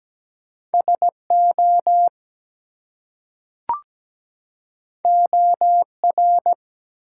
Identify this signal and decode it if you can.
Morse code, heard as a keyed tone, about 17 wpm, 700 Hz. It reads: SO OR
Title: Two Letter Words - 17wpm